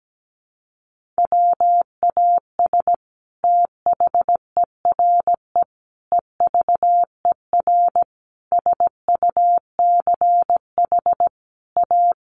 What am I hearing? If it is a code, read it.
Morse code, heard as a keyed tone, about 17 wpm, 700 Hz. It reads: WAS THERE EVER SUCH A
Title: Sentences from Top 400 Words 17wpm